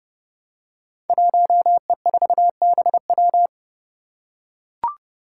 Morse code, heard as a keyed tone, about 30 wpm, 700 Hz. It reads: 1E4BW
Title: Five Challenging Letter-Number Combinations 30wpm